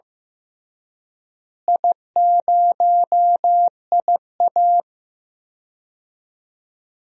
Morse code, heard as a keyed tone, about 15 wpm, 700 Hz. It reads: I0IA
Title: DXCC-100 Call Signs - 4 Characters - Repeated 2x - 15wpm